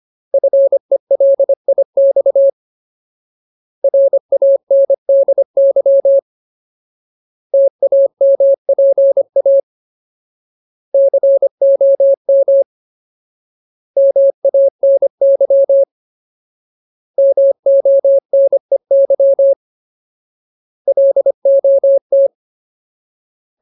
Morse code, heard as a keyed tone, about 25 wpm, 550 Hz. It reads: FELIX RANDY TAMPA COM MANY MONEY LOT
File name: Ultimate Word List / 25WPM Ultimate